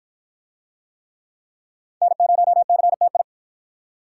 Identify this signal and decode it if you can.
Morse code, heard as a keyed tone, about 53 wpm, 700 Hz. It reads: N0CTI